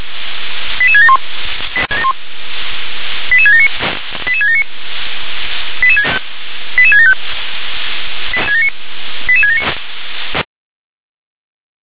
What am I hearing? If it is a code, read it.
Five-tone selective call (ZVEI), heard as quick runs of five tones, tones of about 70 ms. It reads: 88671 88679 88679 88675 88679